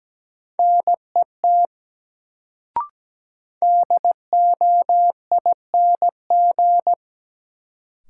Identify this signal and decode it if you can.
Morse code, heard as a keyed tone, about 17 wpm, 700 Hz. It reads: NET DOING
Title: Sentences from Top 1400 Words - New Word Drill - 17wpm